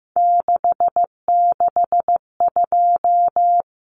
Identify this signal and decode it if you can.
Morse code, heard as a keyed tone, about 15 wpm, 700 Hz. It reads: 662
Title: Common Cut Numbers - 3-Digit 15wpm